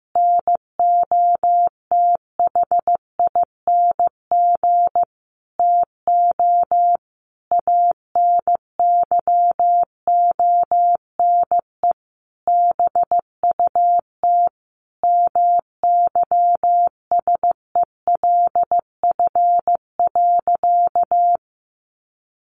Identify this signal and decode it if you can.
Morse code, heard as a keyed tone, about 15 wpm, 700 Hz. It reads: NOTHING TO ANYONE BUT MYSELF.
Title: Sentences from Top 1300 Words 15wpm